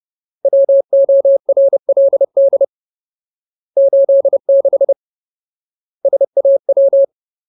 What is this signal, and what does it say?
Morse code, heard as a keyed tone, about 30 wpm, 550 Hz. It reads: WORLD 86 SAW
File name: Ultimate Word List / 30WPM Ultimate